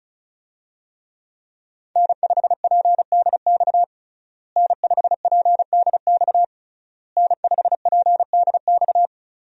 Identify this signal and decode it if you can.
Morse code, heard as a keyed tone, about 36 wpm, 700 Hz. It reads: N5PDX N5PDX N5PDX
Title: USA Call Signs - 5 Characters - Repeated 3x - 35wpm